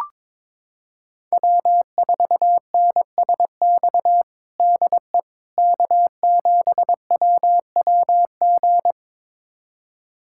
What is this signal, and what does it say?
Morse code, heard as a keyed tone, about 22 wpm, 700 Hz. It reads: W4NSX DE K7WWG
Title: USA Callsign DE Callsign 22wpm